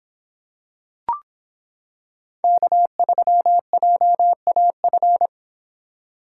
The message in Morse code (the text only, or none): K3JAF